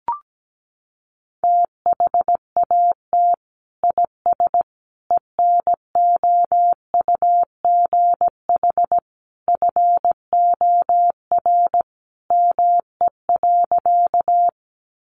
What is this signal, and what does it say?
Morse code, heard as a keyed tone, about 17 wpm, 700 Hz. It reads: THAT IS ENOUGH FOR ME.
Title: Sentences from Top 400 Words 17wpm